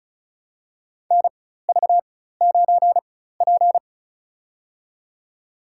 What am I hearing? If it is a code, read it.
Morse code, heard as a keyed tone, about 34 wpm, spaced as for 16 wpm, 700 Hz. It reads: NV9P
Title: USA Call Signs - 4 Characters - 4x Character Spacing - 35wpm